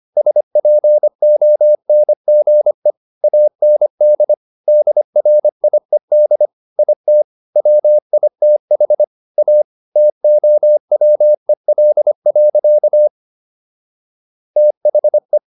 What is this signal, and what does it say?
Morse code, heard as a keyed tone, about 25 wpm, 600 Hz. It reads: SPONGE AND DRIED IT WITH A TOWEL. THE